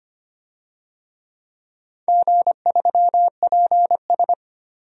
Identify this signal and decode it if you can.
Morse code, heard as a keyed tone, about 25 wpm, 700 Hz. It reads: G3PS
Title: DXCC-100 Call Signs - 4 Characters - 25wpm